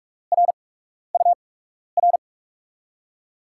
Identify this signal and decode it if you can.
Morse code, heard as a keyed tone, about 44 wpm, spaced as for 12 wpm, 700 Hz. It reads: RUR
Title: USA Call Signs - 8x Character Spacing - 45wpm